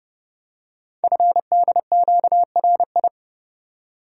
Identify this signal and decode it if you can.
Morse code, heard as a keyed tone, about 30 wpm, 700 Hz. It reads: FDQRI
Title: Code Groups - 5 Letters - 30wpm